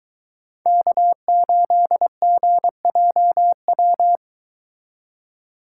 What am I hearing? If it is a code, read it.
Morse code, heard as a keyed tone, about 23 wpm, 700 Hz. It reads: K8GJW